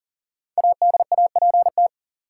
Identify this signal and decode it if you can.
Morse code, heard as a keyed tone, about 40 wpm, 700 Hz. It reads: ADAPT
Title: Five Letter Words - 40wpm